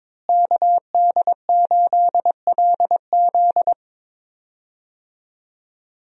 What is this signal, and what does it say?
Morse code, heard as a keyed tone, about 22 wpm, 700 Hz. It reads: KD8LZ
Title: USA Call Signs - 5 Characters - Repeated 3x - 22wpm